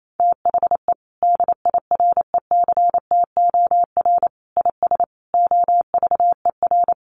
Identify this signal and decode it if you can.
Morse code, heard as a keyed tone, about 28 wpm, 700 Hz. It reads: THE DIRECTOR IS OVER
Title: Sentences from Top 1300 Words 28wpm